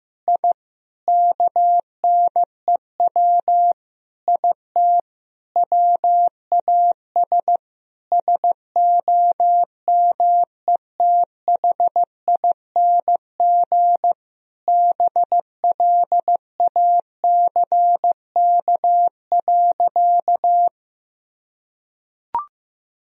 Morse code, heard as a keyed tone, about 15 wpm, 700 Hz. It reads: I KNEW IT WAS SOMETHING BLACK.
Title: Sentences from Top 1000 Words - Review Part 2 - 15wpm